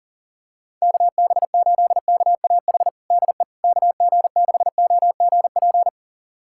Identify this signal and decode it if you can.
Morse code, heard as a keyed tone, about 37 wpm, 700 Hz. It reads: KB8KAH DE KG6OGP
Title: USA Callsign DE Callsign - 6 Characters - 40wpm